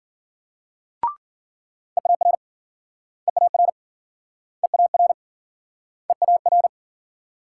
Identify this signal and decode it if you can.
Morse code, heard as a keyed tone, about 52 wpm, 700 Hz. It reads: EAR EAR EAR EAR